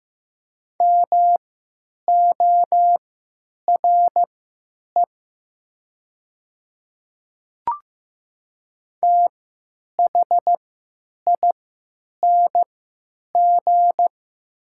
Morse code, heard as a keyed tone, about 15 wpm, spaced as for 8 wpm, 700 Hz. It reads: MORE THING
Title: 3x Character Spacing - Top 500 Words - 15wpm